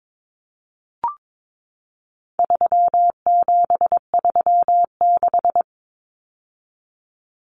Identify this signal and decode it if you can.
Morse code, heard as a keyed tone, about 22 wpm, 700 Hz. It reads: 3736
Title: Numbers - 4-Digit 22wpm